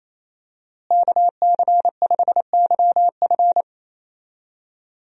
Morse code, heard as a keyed tone, about 28 wpm, 700 Hz. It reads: KC5YF
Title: USA Call Signs - 5 Characters - Repeated 2x - 28wpm